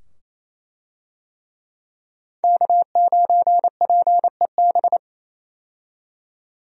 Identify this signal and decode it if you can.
Morse code, heard as a keyed tone, about 28 wpm, 700 Hz. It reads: K9PEB
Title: USA Call Signs - 5 Characters - 28wpm